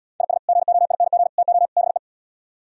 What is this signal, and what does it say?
Morse code, heard as a keyed tone, about 77 wpm, 690 Hz. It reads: IS GOING TO BE